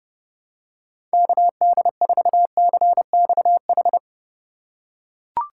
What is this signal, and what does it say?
Morse code, heard as a keyed tone, about 30 wpm, 700 Hz. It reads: KD4CXH
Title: USA Call Signs - 6 Characters - 2x Character Spacing - 30wpm